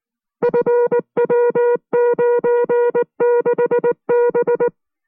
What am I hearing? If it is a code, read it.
Morse code, heard as a keyed tone, about 19 wpm, 480 Hz. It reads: FW96B